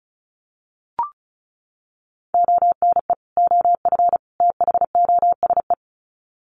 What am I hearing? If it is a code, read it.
Morse code, heard as a keyed tone, about 34 wpm, 700 Hz. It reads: ONE OF THOSE
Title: Sets of 3 Words 35wpm